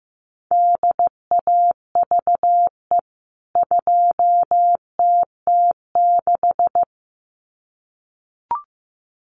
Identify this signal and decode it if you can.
Morse code, heard as a keyed tone, about 15 wpm, 700 Hz. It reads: DAVE 2TT6
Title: CWT Exchange 15wpm